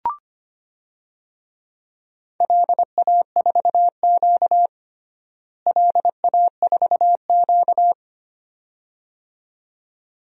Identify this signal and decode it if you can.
Morse code, heard as a keyed tone, about 25 wpm, 700 Hz. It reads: LA4Q LA4Q
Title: DXCC-100 Call Signs - 4 Characters - Repeated 2x - 25wpm